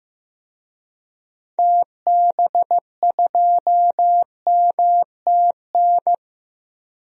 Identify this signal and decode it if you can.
Morse code, heard as a keyed tone, about 15 wpm, 700 Hz. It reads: TB2MTN
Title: DXCC-100 Call Signs - 6 Characters - Repeated 2x - 15wpm